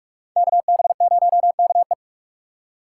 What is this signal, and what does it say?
Morse code, heard as a keyed tone, about 44 wpm, 700 Hz. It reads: KB0KE
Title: USA Call Signs - 5 Characters - 2x Character Spacing - 45wpm